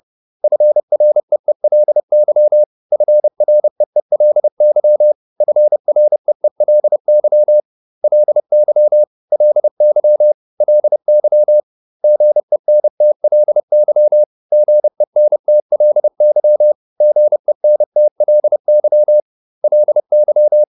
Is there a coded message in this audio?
Yes: Morse code from 0.4 to 20.6 s, heard as a keyed tone, about 30 wpm, 600 Hz. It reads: FREELY FREELY FREELY LY LY LY GENTLY GENTLY GENTLY LY